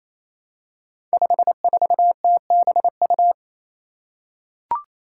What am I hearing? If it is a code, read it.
Morse code, heard as a keyed tone, about 27 wpm, 700 Hz. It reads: 54TBU
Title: Five Challenging Letter-Number Combinations 28wpm